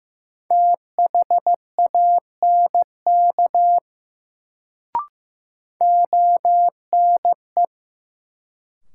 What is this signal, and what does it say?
Morse code, heard as a keyed tone, about 15 wpm, 700 Hz. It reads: THANK ONE
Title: Sets of 2 Words - New Word Drill - 15wpm